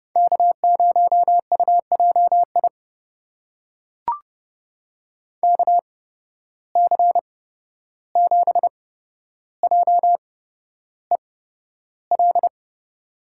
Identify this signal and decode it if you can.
Morse code, heard as a keyed tone, about 30 wpm, spaced as for 12 wpm, 700 Hz. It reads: K0UJI KC7JEL
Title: USA Call Signs - 8x Character Spacing - 30wpm